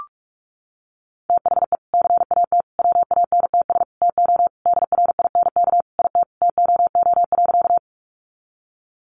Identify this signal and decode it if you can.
Morse code, heard as a keyed tone, about 44 wpm, 700 Hz. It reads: THE CAT WANTS TO DRINK IT TOO.
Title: Additional Easy Phrases 45wpm